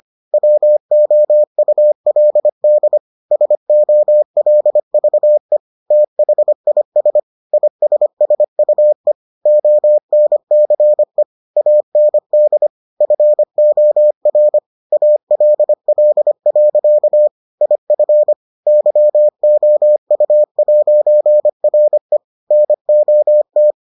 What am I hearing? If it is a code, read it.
Morse code, heard as a keyed tone, about 25 wpm, 600 Hz. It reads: WOULD SOLVE THIS ISSUE ONCE AND FOR ALL. IF YOU'RE NOT